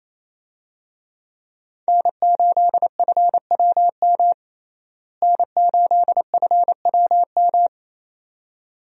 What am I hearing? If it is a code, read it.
Morse code, heard as a keyed tone, about 28 wpm, 700 Hz. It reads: N8FWM N8FWM
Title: USA Call Signs - 5 Characters - Repeated 2x - 28wpm